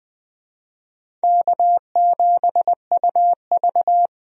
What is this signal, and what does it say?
Morse code, heard as a keyed tone, about 20 wpm, 700 Hz. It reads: K7UV
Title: USA Call Signs - 4 Characters - 20wpm